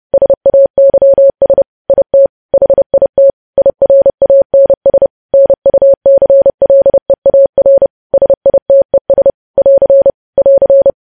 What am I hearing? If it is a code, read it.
Morse code, heard as a keyed tone, about 30 wpm, 550 Hz. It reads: SAYS IT HIT IRANS NUCLEAR SITES <AR> <AR>
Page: news headlines in morse code at 30 wpm